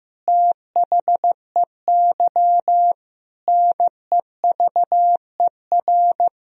Morse code, heard as a keyed tone, about 15 wpm, 700 Hz. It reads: THEY NEVER
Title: Sentences from Top 1000 Words - Review Part 3 - 15wpm